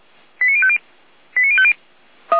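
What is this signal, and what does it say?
Five-tone selective call (ZVEI), heard as quick runs of five tones, tones of about 70 ms. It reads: 89050 89050
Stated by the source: ZVEI five-tone selective call received on an SDR.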